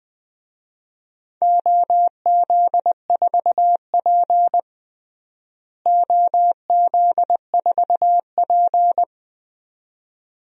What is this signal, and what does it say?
Morse code, heard as a keyed tone, about 20 wpm, 700 Hz. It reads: OZ4P OZ4P
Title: DXCC-100 Call Signs - 4 Characters - Repeated 2x - 20wpm